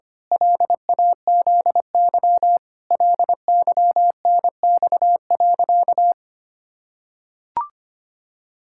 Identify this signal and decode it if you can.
Morse code, heard as a keyed tone, about 25 wpm, 700 Hz. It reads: LAZY LYNX.
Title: Pangram Challenge v2 - 25wpm